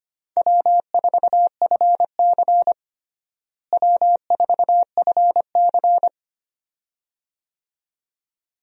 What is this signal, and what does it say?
Morse code, heard as a keyed tone, about 25 wpm, 700 Hz. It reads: W4FC W4FC
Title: USA Call Signs - 4 Characters - Repeated 2x - 25wpm